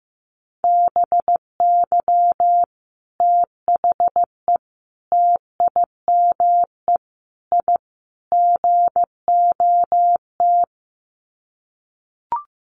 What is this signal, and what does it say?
Morse code, heard as a keyed tone, about 15 wpm, 700 Hz. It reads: BY THE TIME I GOT
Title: Sets of 5 Words Encore2 15wpm